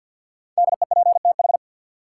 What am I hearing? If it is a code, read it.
Morse code, heard as a keyed tone, about 52 wpm, 700 Hz. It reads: DEPTH
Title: Sentences from Top 1200 Words - New Word Drill Alt - 50wpm